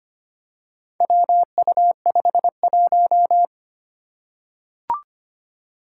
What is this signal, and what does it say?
Morse code, heard as a keyed tone, about 25 wpm, 700 Hz. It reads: WU51
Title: Four Challenging Letter-Number Combinations 25wpm